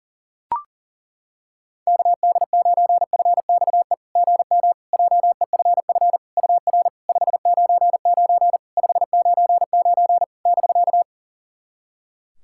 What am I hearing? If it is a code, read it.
Morse code, heard as a keyed tone, about 40 wpm, 690 Hz. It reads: KD9FXE GM JEFF UR 599 599 <BK>
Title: SOTA QSO Practice 40wpm